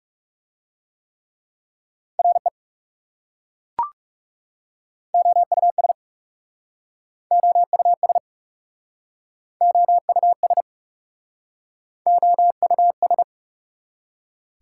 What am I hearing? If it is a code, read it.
Morse code, heard as a keyed tone, about 42 wpm, 690 Hz. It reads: RE OUS OUS OUS OUS